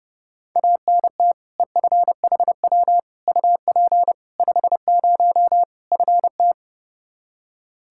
Morse code, heard as a keyed tone, about 32 wpm, 700 Hz. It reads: ANT EFHW UP 50 FT
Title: Practice QSO - Exchange 2 - Info - 30wpm